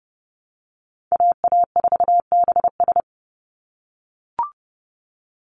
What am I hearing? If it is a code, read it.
Morse code, heard as a keyed tone, about 30 wpm, 700 Hz. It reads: AA4BS
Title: USA Call Signs 30wpm